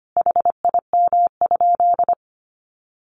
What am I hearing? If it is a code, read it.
Morse code, heard as a keyed tone, about 25 wpm, 700 Hz. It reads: HIM?
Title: Sentences from Top 100 Words 25wpm